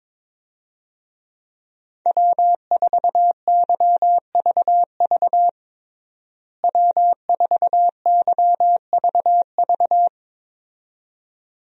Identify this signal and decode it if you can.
Morse code, heard as a keyed tone, about 22 wpm, 700 Hz. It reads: W4YVV W4YVV
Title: USA Call Signs - 5 Characters - Repeated 2x - 22wpm